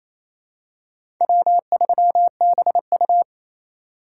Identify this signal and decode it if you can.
Morse code, heard as a keyed tone, about 28 wpm, 700 Hz. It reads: W3BU
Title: USA Call Signs - 4 Characters - 8x Character Spacing - 28wpm